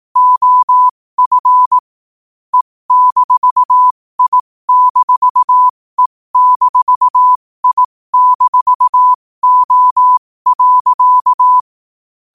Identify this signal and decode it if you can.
Morse code, heard as a keyed tone, about 18 wpm, spaced as for 15 wpm, 1000 Hz. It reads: OF E-I-E-I-O.
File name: qotdfiles / 15wpm